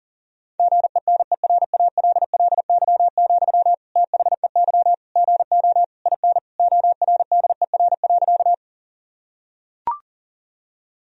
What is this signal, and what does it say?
Morse code, heard as a keyed tone, about 37 wpm, 700 Hz. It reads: GENERALLY, THEY GO IN ORDER.
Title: Sentences from Top 1000 Words - Review Part 4 - 40wpm